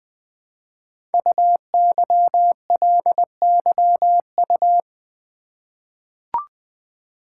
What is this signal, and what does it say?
Morse code, heard as a keyed tone, about 20 wpm, 700 Hz. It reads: UYLYU
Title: Code Groups - 5 Letters - 20wpm